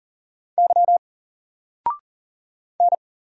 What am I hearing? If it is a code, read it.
Morse code, heard as a keyed tone, about 40 wpm, 700 Hz. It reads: Y N
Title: Single Letter-Number Alt 40wpm